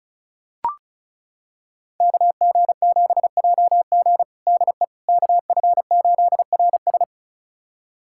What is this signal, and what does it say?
Morse code, heard as a keyed tone, about 36 wpm, 700 Hz. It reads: KG7JG DE KF8RS
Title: USA Callsign DE Callsign - 5 Characters - 35wpm